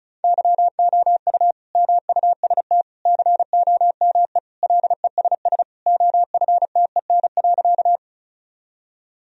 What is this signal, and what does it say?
Morse code, heard as a keyed tone, about 34 wpm, 700 Hz. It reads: YOU MUST COME LESS OFTEN.
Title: Sentences from Top 1000 Words - Review Part 2 - 35wpm